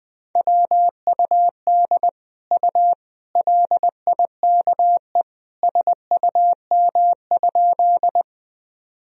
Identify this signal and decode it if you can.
Morse code, heard as a keyed tone, about 21 wpm, 700 Hz. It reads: WUD U LIKE SUM?
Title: Sentences from Top 100 Words and Abbreviations 20wpm